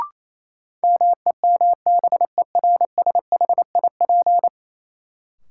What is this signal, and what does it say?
Morse code, heard as a keyed tone, about 28 wpm, 700 Hz. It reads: MEMBERSHIP
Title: Sentences from Top 1400 Words - New Word Drill - 28wpm